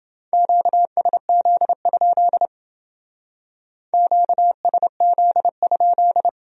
Morse code, heard as a keyed tone, about 28 wpm, 700 Hz. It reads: QSZ? QSZ?